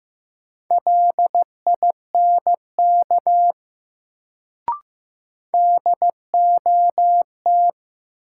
Morse code, heard as a keyed tone, about 15 wpm, 700 Hz. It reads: LINK DOT
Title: Sentences from Top 1500 Words - New Word Drill Alt - 15wpm